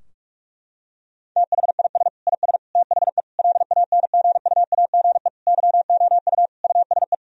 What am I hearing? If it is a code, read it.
Morse code, heard as a keyed tone, about 46 wpm, 700 Hz. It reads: THIS IS THE LANGUAGE YOU USE
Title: Sentences from Top 1000 Words - Review Part 3 - 45wpm